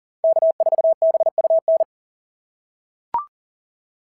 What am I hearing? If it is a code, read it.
Morse code, heard as a keyed tone, about 37 wpm, 660 Hz. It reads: K4BUN
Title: SOTA QSO Practice 40wpm